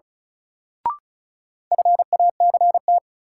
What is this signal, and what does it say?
Morse code, heard as a keyed tone, about 36 wpm, 700 Hz. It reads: FACT
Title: Sentences from Top 500 Words - New Word Drill - 35wpm